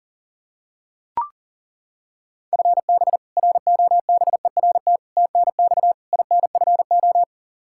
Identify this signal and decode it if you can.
Morse code, heard as a keyed tone, about 40 wpm, 700 Hz. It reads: FB ROBERT TNX INFO
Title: K1USN SST QSO Practice 40wpm